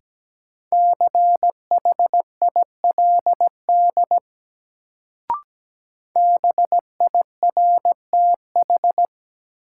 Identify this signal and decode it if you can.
Morse code, heard as a keyed tone, about 17 wpm, 700 Hz. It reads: CHILD BIRTH
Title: Sentences from Top 800 Words - New Word Drill - 17wpm